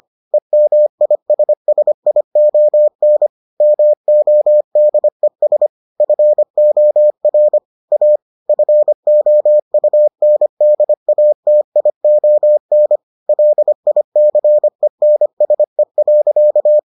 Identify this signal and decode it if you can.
Morse code, heard as a keyed tone, about 25 wpm, 600 Hz. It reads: EMISSION MODES FOR A FOUNDATION LICENSE.